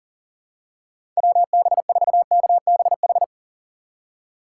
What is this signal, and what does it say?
Morse code, heard as a keyed tone, about 40 wpm, 700 Hz. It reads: WB4KBH
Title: USA Call Signs - 6 Characters - 4x Character Spacing - 40wpm